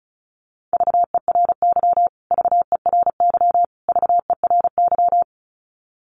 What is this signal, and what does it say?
Morse code, heard as a keyed tone, about 34 wpm, 700 Hz. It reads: VERY VERY VERY
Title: Top 500 Words Repeated 35wpm